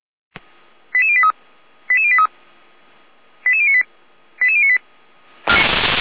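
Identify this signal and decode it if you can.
Five-tone selective call (ZVEI), heard as quick runs of five tones, tones of about 70 ms. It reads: 88083 88083 88098 88098
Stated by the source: Five-tone ZVEI selective call received on an SDR.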